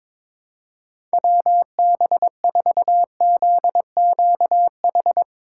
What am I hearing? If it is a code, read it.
Morse code, heard as a keyed tone, about 21 wpm, 700 Hz. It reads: WB4ZQH
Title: USA Call Signs - 6 Characters - 3x Character Spacing - 22wpm